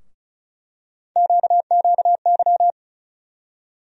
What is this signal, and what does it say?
Morse code, heard as a keyed tone, about 36 wpm, 700 Hz. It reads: QQY
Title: USA Call Sign Postfixes - 3 Letters 35wpm